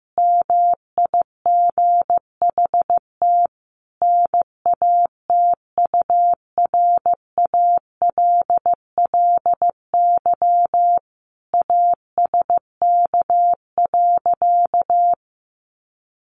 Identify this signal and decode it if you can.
Morse code, heard as a keyed tone, about 15 wpm, 700 Hz. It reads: MIGHT NATURALLY ASK.
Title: Sentences from Top 1300 Words 15wpm